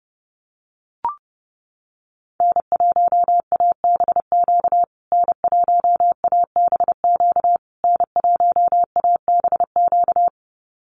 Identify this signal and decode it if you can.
Morse code, heard as a keyed tone, about 30 wpm, 700 Hz. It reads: N1ABQ N1ABQ N1ABQ
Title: USA Call Signs Repeated 3x 30wpm